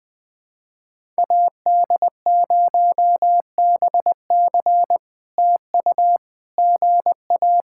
Morse code, heard as a keyed tone, about 20 wpm, 700 Hz. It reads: AD0BC TU GA
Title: POTA QSO Practice - 20wpm